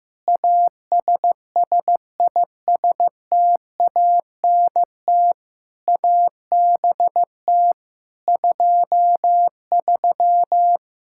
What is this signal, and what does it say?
Morse code, heard as a keyed tone, about 16 wpm, 700 Hz. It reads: ASSISTANT ABT 23
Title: Practice QSO - Exchange 3 - More Info - 15wpm